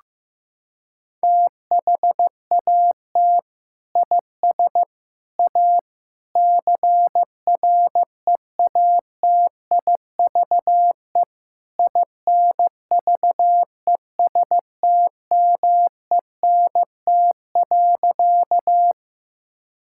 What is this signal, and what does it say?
Morse code, heard as a keyed tone, about 15 wpm, 700 Hz. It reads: THAT IS A CREATIVE INVESTMENT.
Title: Sentences from Top 1000 Words - Review Part 5 - 15wpm